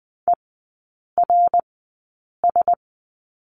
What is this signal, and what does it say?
Morse code, heard as a keyed tone, about 20 wpm, spaced as for 8 wpm, 700 Hz. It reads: ERS
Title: 053 Lesson 10e - Letter r - Three Characters 20wpm